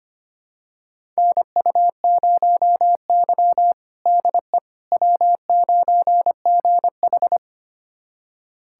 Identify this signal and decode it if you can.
Morse code, heard as a keyed tone, about 25 wpm, 700 Hz. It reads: NU0Y DE W9GH
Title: USA Callsign DE Callsign - 4 Characters - 25wpm